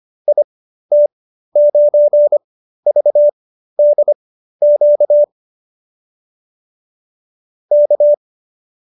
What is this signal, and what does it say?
Morse code, heard as a keyed tone, about 25 wpm, spaced as for 13 wpm, 590 Hz. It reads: IT9VDQ K